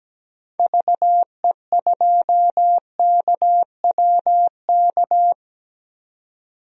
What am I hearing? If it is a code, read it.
Morse code, heard as a keyed tone, about 17 wpm, 700 Hz. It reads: VE2KWK
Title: DXCC-100 Call Signs - 6 Characters - 17wpm